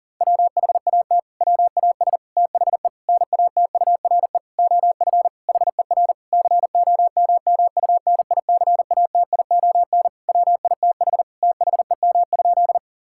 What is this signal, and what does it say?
Morse code, heard as a keyed tone, about 40 wpm, 700 Hz. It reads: WHAT WAS THE NATURE OF HER COMMUNICATION WITH THEM?
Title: Sentences from Top 1000 Words - Review Part 6 - 40wpm